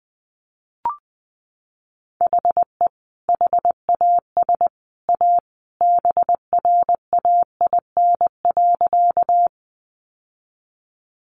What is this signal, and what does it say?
Morse code, heard as a keyed tone, about 20 wpm, 700 Hz. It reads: HE HAS A BRAIN.
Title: Sentences from Top 1100 Words 20wpm